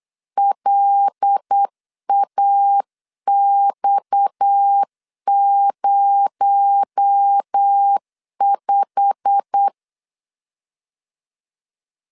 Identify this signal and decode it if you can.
Morse code, heard as a keyed tone, about 8 wpm, 800 Hz. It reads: LAX05
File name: LEZIONI / 9 2867 8wpm 3440